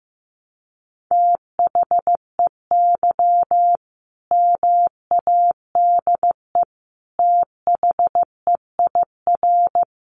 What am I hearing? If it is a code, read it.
Morse code, heard as a keyed tone, about 15 wpm, 700 Hz. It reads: THEY MADE THEIR
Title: Sentences from Top 1400 Words 15wpm